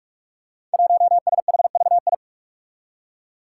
Morse code, heard as a keyed tone, about 46 wpm, 700 Hz. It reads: 1SHVI
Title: Five Challenging Letter-Number Combinations 45wpm